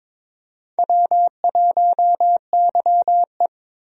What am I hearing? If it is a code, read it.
Morse code, heard as a keyed tone, about 22 wpm, 700 Hz. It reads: W1YE
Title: USA Call Signs - 4 Characters - 22wpm